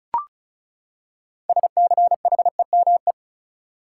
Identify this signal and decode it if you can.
Morse code, heard as a keyed tone, about 34 wpm, 700 Hz. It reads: SCHEME
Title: Six Letter Words - 35wpm